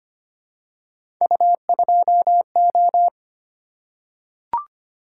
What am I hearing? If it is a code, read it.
Morse code, heard as a keyed tone, about 25 wpm, 700 Hz. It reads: U2O
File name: Three Letter-Number Combinations - 25wpm